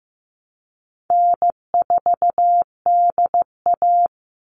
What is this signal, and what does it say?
Morse code, heard as a keyed tone, about 15 wpm, 700 Hz. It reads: N4DA
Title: NA SOTA Chaser Callsigns 15wpm